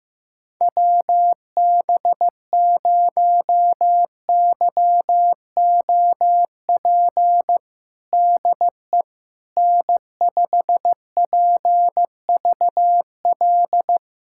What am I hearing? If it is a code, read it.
Morse code, heard as a keyed tone, about 15 wpm, 700 Hz. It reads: WB0YOP DE N5PVL
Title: USA Callsign DE Callsign 15wpm